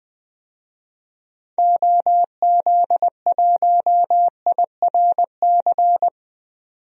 Morse code, heard as a keyed tone, about 20 wpm, 700 Hz. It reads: OZ1IRC
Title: DXCC-100 Call Signs - 6 Characters - 20wpm